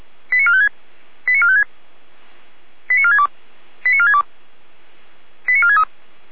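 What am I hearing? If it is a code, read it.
Five-tone selective call (ZVEI), heard as quick runs of five tones, tones of about 80 ms. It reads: 89456 89456 89462 89462 89463
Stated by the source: ZVEI five-tone selective call received on SDR.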